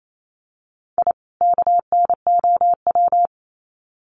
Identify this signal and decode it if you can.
Morse code, heard as a keyed tone, about 28 wpm, 700 Hz. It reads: I KNOW
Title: Sets of 2 Words 28wpm